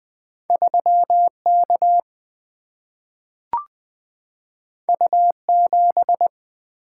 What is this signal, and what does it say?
Morse code, heard as a keyed tone, about 20 wpm, 700 Hz. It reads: 3 K U 7
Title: Single-Letter Single-Number 20wpm